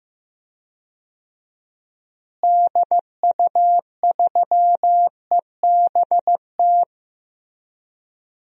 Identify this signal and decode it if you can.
Morse code, heard as a keyed tone, about 15 wpm, 700 Hz. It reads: DU3EBT
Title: DXCC-100 Call Signs - 6 Characters - Repeated 2x - 15wpm